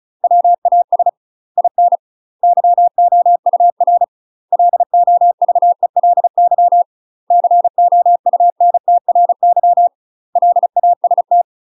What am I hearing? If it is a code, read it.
Morse code, heard as a keyed tone, about 36 wpm, 700 Hz. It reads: WAS IN YOUR LOVELY COUNTRY LAST
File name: UK MASTER QSOs TRAINING/35WPM